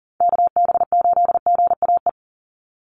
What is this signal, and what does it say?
Morse code, heard as a keyed tone, about 40 wpm, 700 Hz. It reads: KB8GAE
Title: Call Signs Repeated 3x - 40wpm